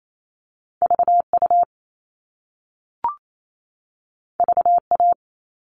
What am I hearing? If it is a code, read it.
Morse code, heard as a keyed tone, about 28 wpm, 700 Hz. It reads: VU VA
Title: All Cut Numbers - 2-Digit 28wpm